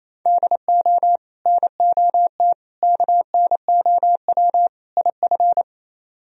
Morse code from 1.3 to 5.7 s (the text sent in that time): NOT KNOW IF